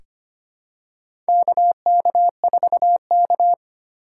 Morse code, heard as a keyed tone, about 25 wpm, 700 Hz. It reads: KK4K
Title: USA Call Signs - 4 Characters - 2x Character Spacing - 25wpm